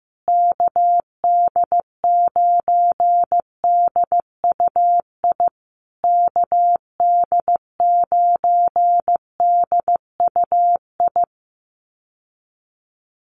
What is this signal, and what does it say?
Morse code, heard as a keyed tone, about 15 wpm, 700 Hz. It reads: KD9DUI KD9DUI
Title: USA Call Signs Repeated 3x 15wpm